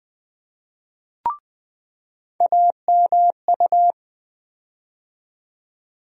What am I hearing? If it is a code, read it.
Morse code, heard as a keyed tone, about 20 wpm, 700 Hz. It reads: AMU
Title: USA Call Sign Postfixes - 3 Letters 20wpm